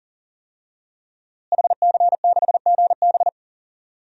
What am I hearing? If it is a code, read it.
Morse code, heard as a keyed tone, about 40 wpm, 700 Hz. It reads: HC6GB
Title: DXCC-100 Call Signs - 5 Characters - Repeated 2x - 40wpm